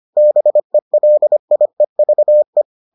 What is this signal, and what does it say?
Morse code, heard as a keyed tone, about 25 wpm, 600 Hz. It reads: BELIEVE